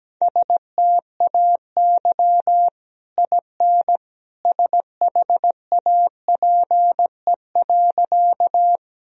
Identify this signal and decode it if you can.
Morse code, heard as a keyed tone, about 17 wpm, 700 Hz. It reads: STAY IN SHAPE.
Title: Sentences from Top 1000 Words - Review Part 3 - 17wpm